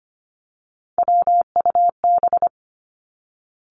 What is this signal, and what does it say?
Morse code, heard as a keyed tone, about 25 wpm, 700 Hz. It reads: WUB
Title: Three Challenging Letter-Number Combinations 25wpm